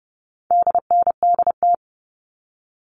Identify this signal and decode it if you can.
Morse code, heard as a keyed tone, about 30 wpm, 700 Hz. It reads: DNDT
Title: All Cut Numbers - 4-Digit 30wpm